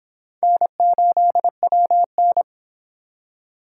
Morse code, heard as a keyed tone, about 26 wpm, 700 Hz. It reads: N8WN